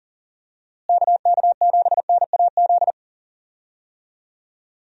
Morse code, heard as a keyed tone, about 39 wpm, 700 Hz. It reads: KK7NAZ
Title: USA Call Signs - 6 Characters - Repeated 3x - 40wpm